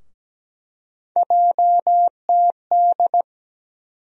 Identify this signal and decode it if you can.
Morse code, heard as a keyed tone, about 17 wpm, 700 Hz. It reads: JTD
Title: USA Call Sign Postfixes - 3 Letters 17wpm